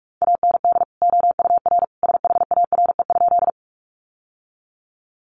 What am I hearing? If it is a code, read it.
Morse code, heard as a keyed tone, about 44 wpm, 700 Hz. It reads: AND OUR SHARE?
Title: Sentences from Top 900 Words 45wpm